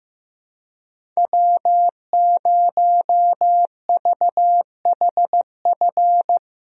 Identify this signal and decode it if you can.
Morse code, heard as a keyed tone, about 16 wpm, 700 Hz. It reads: W0VHF
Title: USA Call Signs - 5 Characters - Repeated 3x - 15wpm